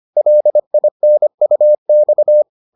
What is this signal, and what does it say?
Morse code, heard as a keyed tone, about 25 wpm, 600 Hz. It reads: LINUX